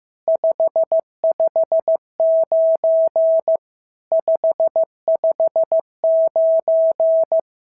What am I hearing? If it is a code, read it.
Morse code, heard as a keyed tone, about 14 wpm, 650 Hz. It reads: 559 559
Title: SOTA QSO Practice 15wpm